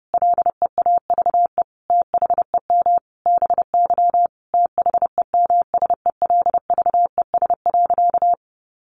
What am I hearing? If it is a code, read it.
Morse code, heard as a keyed tone, about 30 wpm, 700 Hz. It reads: LEAVE THEM BY THEMSELVES.
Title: Sentences from Top 800 Words 30wpm